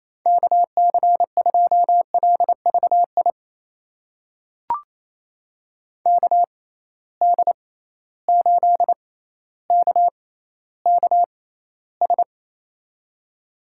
Morse code, heard as a keyed tone, about 28 wpm, spaced as for 14 wpm, 700 Hz. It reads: KC2LVI KD8KKS
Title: USA Call Signs - 6x Character Spacing - 28wpm